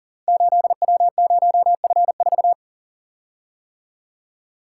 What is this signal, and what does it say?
Morse code, heard as a keyed tone, about 40 wpm, 700 Hz. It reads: 8W0F4
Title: Code Groups - 5 Letter:Numbers - 40wpm